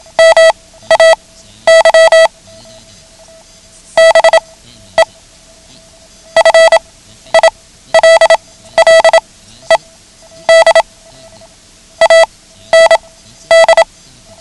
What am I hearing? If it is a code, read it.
Morse code, heard as a keyed tone, about 27 wpm, spaced as for 13 wpm, 650 Hz. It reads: MAY BE FILLED AND
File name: ham radio / CodeQuick / Code Quick 09